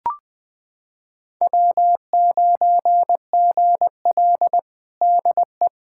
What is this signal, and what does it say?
Morse code, heard as a keyed tone, about 21 wpm, 700 Hz. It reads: W9GL DE
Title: Practice QSO - Exchange 4 - Ending - 20wpm